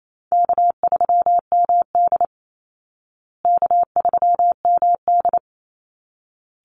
Morse code, heard as a keyed tone, about 28 wpm, 700 Hz. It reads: K3MD K3MD
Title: Call Signs Repeated 3x - 28wpm